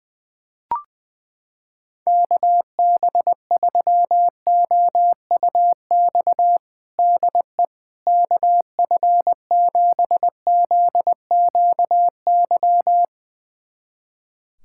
Morse code, heard as a keyed tone, about 20 wpm, 700 Hz. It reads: KB3OUX DE KF7ZQY
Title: USA Callsign DE Callsign - 6 Characters - 20wpm